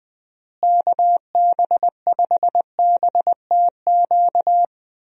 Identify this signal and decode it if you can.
Morse code, heard as a keyed tone, about 21 wpm, 700 Hz. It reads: KB5BTQ
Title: USA Call Signs - 6 Characters - Repeated 3x - 20wpm